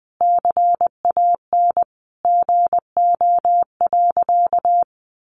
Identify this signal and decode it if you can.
Morse code, heard as a keyed tone, about 20 wpm, 700 Hz. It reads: CAN GO.
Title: Sentences from Top 100 Words 20wpm